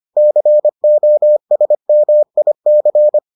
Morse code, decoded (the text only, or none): COSMIC